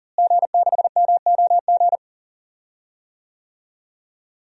Morse code, heard as a keyed tone, about 40 wpm, 700 Hz. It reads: G6MOG
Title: DXCC-100 Call Signs - 5 Characters - Repeated 3x - 40wpm